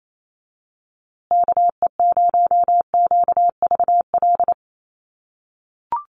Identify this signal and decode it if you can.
Morse code, heard as a keyed tone, about 28 wpm, 700 Hz. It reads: KE0QVL
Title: USA Call Signs Repeated 2x 28wpm